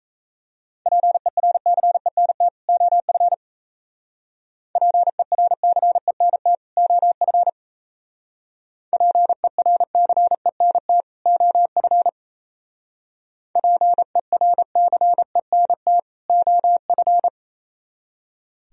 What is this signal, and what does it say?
Morse code, heard as a keyed tone, about 34 wpm, 700 Hz. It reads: PERCENT OF PERCENT OF PERCENT OF PERCENT OF
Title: Speed-Racing - Sets of 2 Words 28wpm